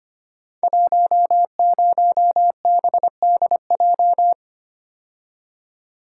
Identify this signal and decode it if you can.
Morse code, heard as a keyed tone, about 25 wpm, 700 Hz. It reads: 10BDJ
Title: Code Groups - 5 Letter:Numbers - 25wpm